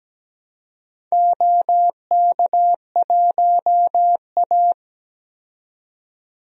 Morse code, heard as a keyed tone, about 17 wpm, 700 Hz. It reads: OK1A
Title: DXCC-100 Call Signs - 4 Characters - 17wpm